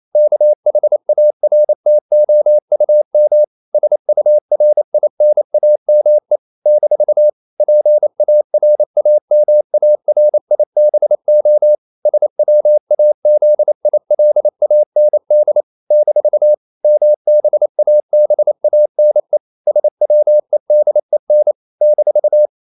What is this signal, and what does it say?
Morse code, heard as a keyed tone, about 28 wpm, 600 Hz. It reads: KHARTOUM SURINAME - PARAMARIBO SWAZILAND - MBABANE SWEDEN -
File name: CW / Practice / NationalCapitals28